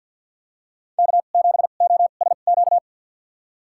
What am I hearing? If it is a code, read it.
Morse code, heard as a keyed tone, about 53 wpm, spaced as for 37 wpm, 700 Hz. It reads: K7OSQ
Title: USA Call Signs - 5 Characters - 2x Character Spacing - 50wpm